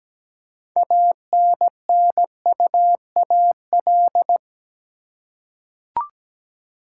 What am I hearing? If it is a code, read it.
Morse code, heard as a keyed tone, about 17 wpm, 700 Hz. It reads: ANNUAL
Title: Six Letter Words - 17wpm